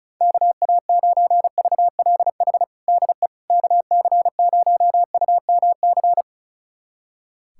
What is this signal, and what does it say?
Morse code, heard as a keyed tone, about 34 wpm, 700 Hz. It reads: KA9VLH DE KC0UMC
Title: USA Callsign DE Callsign - 6 Characters - 35wpm